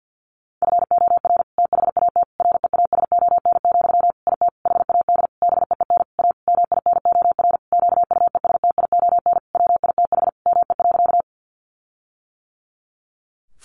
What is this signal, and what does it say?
Morse code, heard as a keyed tone, about 50 wpm, 700 Hz. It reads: FOR THAT REASON, IT HAD BEEN A MINOR QUESTION WITH ME.
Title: Sentences from Top 1500 Words 50wpm